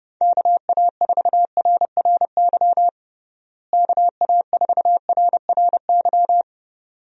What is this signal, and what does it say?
Morse code, heard as a keyed tone, about 30 wpm, 700 Hz. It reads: KA4RRY KA4RRY
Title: USA Call Signs - 6 Characters - Repeated 2x - 30wpm